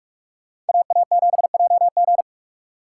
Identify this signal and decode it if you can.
Morse code, heard as a keyed tone, about 57 wpm, 700 Hz. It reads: AA7JG